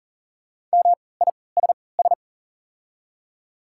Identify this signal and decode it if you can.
Morse code, heard as a keyed tone, about 40 wpm, spaced as for 23 wpm, 690 Hz. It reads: MISS
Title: 3x Character Spacing - Top 500 Words - 40wpm